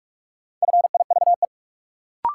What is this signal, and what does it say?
Morse code, heard as a keyed tone, about 46 wpm, 700 Hz. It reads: FIVE
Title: Sentences from Top 400 Words - New Word Drill - 45wpm